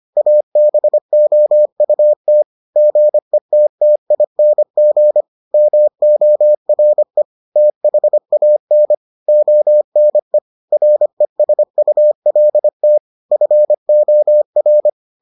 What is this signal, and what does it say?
Morse code, heard as a keyed tone, about 25 wpm, 600 Hz. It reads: ABOUT GETTING MORE THAN ONE RESULT FOR